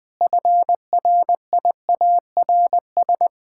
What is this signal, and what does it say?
Morse code, heard as a keyed tone, about 21 wpm, 700 Hz. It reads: FRIARS
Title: Pangram Challenge v2 - 20wpm